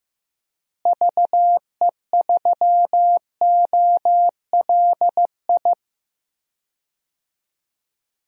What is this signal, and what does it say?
Morse code, heard as a keyed tone, about 15 wpm, 700 Hz. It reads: VE3OLI
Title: DXCC-100 Call Signs - 6 Characters - Repeated 2x - 15wpm